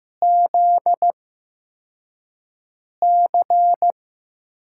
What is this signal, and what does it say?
Morse code, heard as a keyed tone, about 15 wpm, spaced as for 4 wpm, 700 Hz. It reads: ZC
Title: USA Call Signs - 8x Character Spacing - 15wpm